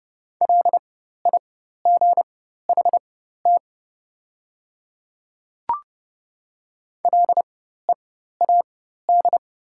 Morse code, heard as a keyed tone, about 30 wpm, spaced as for 14 wpm, 700 Hz. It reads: LIGHT LEAD
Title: 4x Character Spacing - Top 500 Words - 30wpm